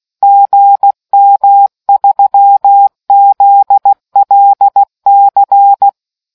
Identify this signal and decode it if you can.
Morse code, heard as a keyed tone, about 16 wpm, 800 Hz. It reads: GM3ZLC